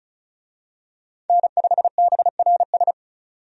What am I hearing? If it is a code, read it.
Morse code, heard as a keyed tone, about 36 wpm, 700 Hz. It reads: N5BRS
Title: USA Call Signs - 5 Characters - 35wpm